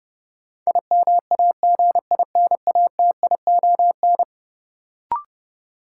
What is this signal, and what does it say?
Morse code, heard as a keyed tone, about 30 wpm, 700 Hz. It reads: IMAGINATION
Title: Sentences from Top 1300 Words - New Word Drill - 30wpm